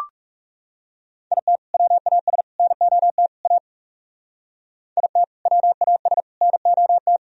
Morse code, heard as a keyed tone, about 40 wpm, 700 Hz. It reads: IT WAS NOT A IT WAS NOT